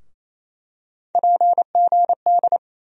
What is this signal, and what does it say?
Morse code, heard as a keyed tone, about 28 wpm, 700 Hz. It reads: PGD
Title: USA Call Sign Postfixes - 3 Letters 28wpm